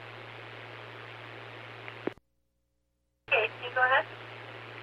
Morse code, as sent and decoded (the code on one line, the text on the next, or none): none